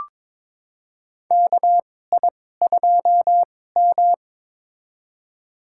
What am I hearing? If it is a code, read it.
Morse code, heard as a keyed tone, about 22 wpm, spaced as for 16 wpm, 700 Hz. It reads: KI2M
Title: USA Call Signs - 2x Character Spacing - 22wpm